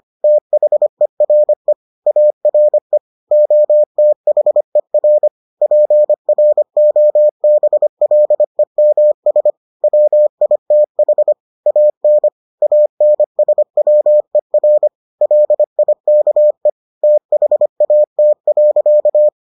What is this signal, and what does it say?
Morse code, heard as a keyed tone, about 25 wpm, 600 Hz. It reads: THERE ARE OTHER PROBLEMS WITH AN ANSWER LIKE THAT.